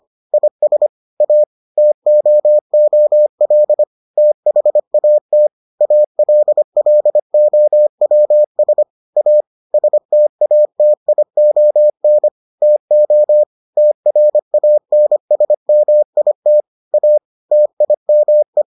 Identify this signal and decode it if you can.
Morse code, heard as a keyed tone, about 25 wpm, 600 Hz. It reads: IS A TOOL THAT ALLOWS A STATION TO TRANSMIT A TIME